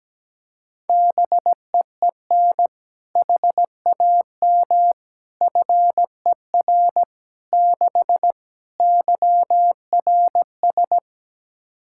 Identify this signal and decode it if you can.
Morse code, heard as a keyed tone, about 17 wpm, 700 Hz. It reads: BEEN HAM FER 6 YRS
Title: K1USN SST QSO Practice 17wpm